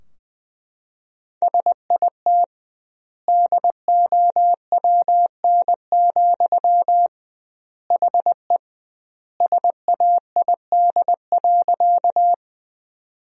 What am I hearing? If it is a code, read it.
Morse code, heard as a keyed tone, about 21 wpm, 700 Hz. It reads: SIT DOWN, HE SAID.
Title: 2x Word-Spacing - Sentences from Top 500 Words - 20wpm